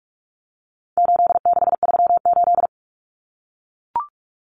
Morse code, heard as a keyed tone, about 44 wpm, 700 Hz. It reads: 8638
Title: Common Cut Numbers - 4-Digit 45wpm